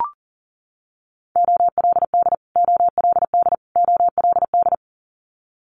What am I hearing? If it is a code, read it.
Morse code, heard as a keyed tone, about 40 wpm, 700 Hz. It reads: OLD OLD OLD
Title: Top 500 Words Repeated 40wpm